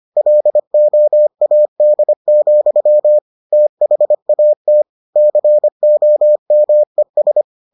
Morse code, heard as a keyed tone, about 25 wpm, 600 Hz. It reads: LOAD, THAT COMES